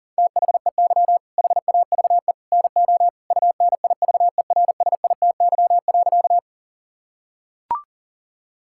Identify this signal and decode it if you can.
Morse code, heard as a keyed tone, about 37 wpm, 700 Hz. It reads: THEY HAVE NO UNIVERSITY.
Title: Sentences from Top 1000 Words - Review Part 5 - 40wpm